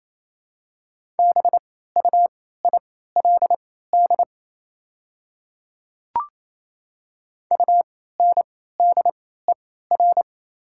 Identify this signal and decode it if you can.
Morse code, heard as a keyed tone, about 28 wpm, spaced as for 16 wpm, 690 Hz. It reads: BUILD UNDER
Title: 3x Character Spacing - Top 200 Words - 28wpm